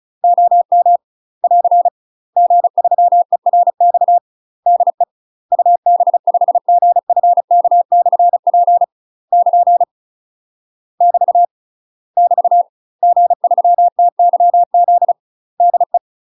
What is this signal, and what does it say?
Morse code, heard as a keyed tone, about 34 wpm, 700 Hz. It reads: OM <AR> G3ERX DE UB5GFK/P <KN> <BT> <BT> G3TYZ DE
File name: UK MASTER QSOs TRAINING / 35WPM / UK MASTER QSOs 0004